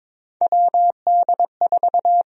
Morse code, heard as a keyed tone, about 22 wpm, 700 Hz. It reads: WD4
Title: USA Call Sign Prefixes - 22wpm